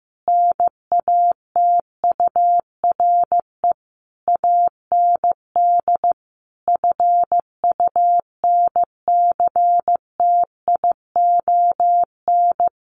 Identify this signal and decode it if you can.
Morse code, heard as a keyed tone, about 15 wpm, 700 Hz. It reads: NATURE AND FUNCTION
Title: Sentences from Top 1100 Words 15wpm